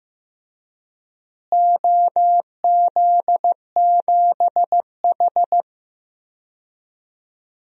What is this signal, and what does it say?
Morse code, heard as a keyed tone, about 14 wpm, 700 Hz. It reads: OZ7H
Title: DXCC-100 Call Signs - 4 Characters - Repeated 2x - 15wpm